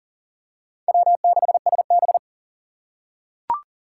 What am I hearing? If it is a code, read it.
Morse code, heard as a keyed tone, about 40 wpm, 700 Hz. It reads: W6SB
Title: USA Call Signs - 4 Characters - 3x Character Spacing - 40wpm